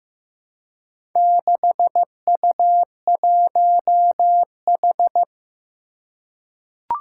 Morse code, heard as a keyed tone, about 15 wpm, 700 Hz. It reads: 6U1H
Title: Four Challenging Letter-Number Combinations 15wpm